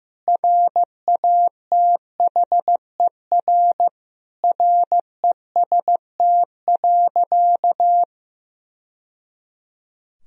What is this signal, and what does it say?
Morse code, heard as a keyed tone, about 16 wpm, 700 Hz. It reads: RATHER REST.
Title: Sentences from Top 1000 Words - Review Part 4 - 15wpm